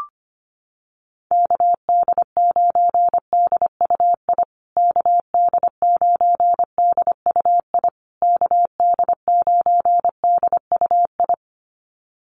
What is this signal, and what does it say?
Morse code, heard as a keyed tone, about 25 wpm, 700 Hz. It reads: KD9DUI KD9DUI KD9DUI
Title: USA Call Signs Repeated 3x 25wpm